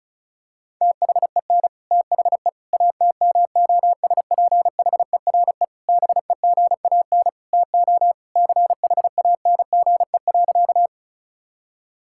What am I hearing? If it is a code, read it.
Morse code, heard as a keyed tone, about 34 wpm, 700 Hz. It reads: THEN THE ATMOSPHERE BEGAN TO CHANGE.
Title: Sentences from Top 1600 Words 35wpm